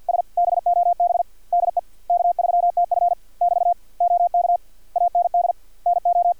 Morse code, heard as a keyed tone, about 50 wpm, 700 Hz. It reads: S BOB DE K3TF <BT> OK AND NO
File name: archived / CRSnet